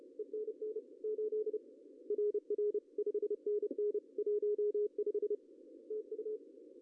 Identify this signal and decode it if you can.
Morse code, heard as a keyed tone, about 32 wpm, 410 Hz. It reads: ENN 8 RR 5NN 15 TU
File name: CW / cqww-cw-k5tr